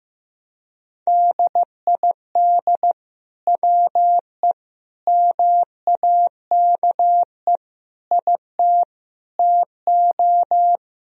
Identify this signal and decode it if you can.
Morse code, heard as a keyed tone, about 16 wpm, 700 Hz. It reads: DID WE MAKE IT TO
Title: Sentences from Top 1000 Words - Review Part 3 - 15wpm